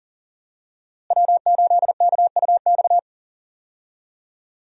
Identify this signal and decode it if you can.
Morse code, heard as a keyed tone, about 39 wpm, 700 Hz. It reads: W8KUX
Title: USA Call Signs - 5 Characters - Repeated 2x - 40wpm